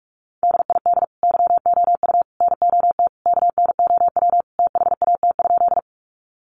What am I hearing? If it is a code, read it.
Morse code, heard as a keyed tone, about 44 wpm, 700 Hz. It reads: DID YOU NOT KNOW THAT?
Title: Sentences from Top 600 Words 45wpm